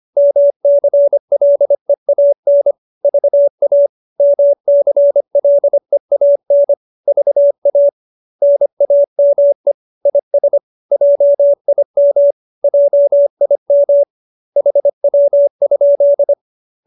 Morse code, heard as a keyed tone, about 25 wpm, 570 Hz. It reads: MCLEAN VA MCLEAN VA NAME IS JIM JIM HW?